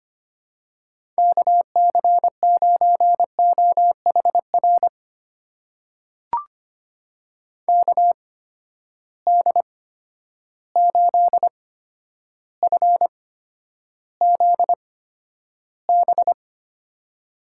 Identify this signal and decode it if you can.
Morse code, heard as a keyed tone, about 25 wpm, spaced as for 11 wpm, 700 Hz. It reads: KC9OHR KD8FZB
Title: USA Call Signs - 6 Characters - 8x Character Spacing - 25wpm